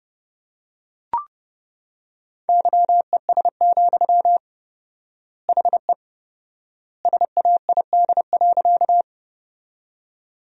Morse code, heard as a keyed tone, about 30 wpm, 700 Hz. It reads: YES, HE SAID.
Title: 4x Word-Spacing - Sentences from Top 500 Words - 30wpm